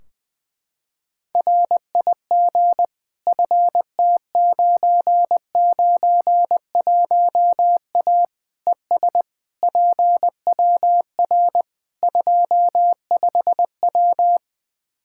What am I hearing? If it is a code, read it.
Morse code, heard as a keyed tone, about 20 wpm, 700 Hz. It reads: RIG FT991A ES PWR 25W
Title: Practice QSO - Exchange 2 - Info - 20wpm